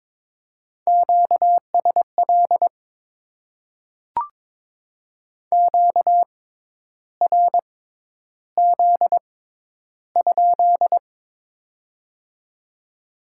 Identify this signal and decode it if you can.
Morse code, heard as a keyed tone, about 22 wpm, spaced as for 10 wpm, 700 Hz. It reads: QSL QRZ?
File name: Top 90 QSO Elements - 6x Character Spacing - 22wpm